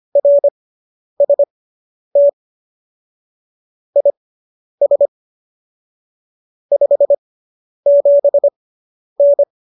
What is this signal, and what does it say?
Morse code, heard as a keyed tone, about 25 wpm, spaced as for 10 wpm, 570 Hz. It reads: RST IS 57N